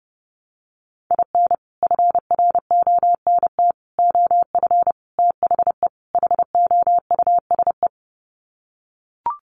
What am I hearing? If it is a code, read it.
Morse code, heard as a keyed tone, about 30 wpm, 700 Hz. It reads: IN FRONT OF THE HOUSE
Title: Sets of 5 Words Encore2 30wpm